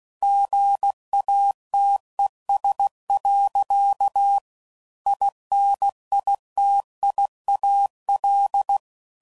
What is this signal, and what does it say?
Morse code, heard as a keyed tone, about 16 wpm, 800 Hz. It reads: GATES. INITIAL
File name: CW Code Groups @15WPM / Technical Subjects / Gates Intro